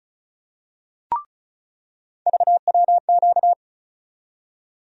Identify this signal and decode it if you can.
Morse code, heard as a keyed tone, about 36 wpm, 700 Hz. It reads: VWQ
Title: USA Call Sign Postfixes - 3 Letters 35wpm